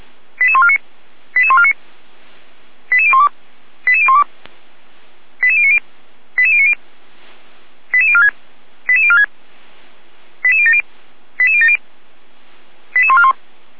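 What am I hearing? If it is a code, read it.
Five-tone selective call (ZVEI), heard as quick runs of five tones, tones of about 70 ms. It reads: 88149 88149 88012 88012 88090 88090 88046 88046 88070 88070 88141
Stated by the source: ZVEI five-tone selective call received on an SDR, fire brigade Feuerwehr Orlamünde.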